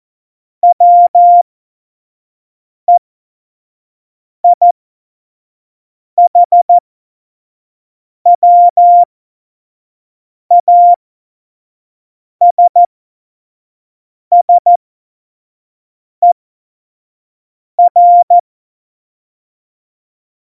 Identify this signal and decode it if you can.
Morse code, heard as a keyed tone, about 14 wpm, spaced as for 5 wpm, 700 Hz. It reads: WEIHWASSER
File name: Morsekurs / Witze / Witz 12. Januar 2021, 25 F70